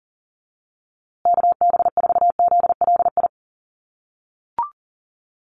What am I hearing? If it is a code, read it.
Morse code, heard as a keyed tone, about 40 wpm, 700 Hz. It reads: KB4ZLI
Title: USA Call Signs Repeated 3x 40wpm